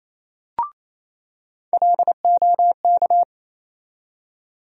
Morse code, heard as a keyed tone, about 28 wpm, 700 Hz. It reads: LOK
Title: USA Call Sign Postfixes - 3 Letters 28wpm